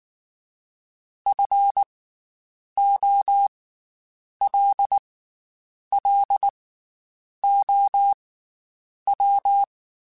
Morse code, heard as a keyed tone, about 19 wpm, 800 Hz. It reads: F O L L O W
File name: Special INGs / Special INGs-mp3s / Slow ing @18